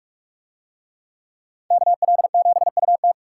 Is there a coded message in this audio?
Yes: Morse code from 1.7 to 3.1 s, heard as a keyed tone, about 40 wpm, 700 Hz. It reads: KL7UT